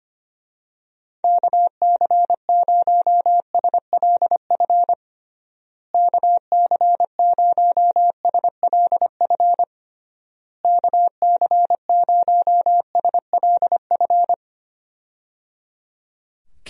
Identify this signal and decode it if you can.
Morse code, heard as a keyed tone, about 25 wpm, 700 Hz. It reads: KC0SLF KC0SLF KC0SLF
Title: USA Call Signs - 6 Characters - Repeated 3x - 25wpm